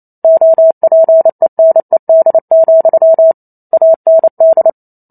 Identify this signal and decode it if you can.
Morse code, heard as a keyed tone, about 28 wpm, 650 Hz. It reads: OPENED, AND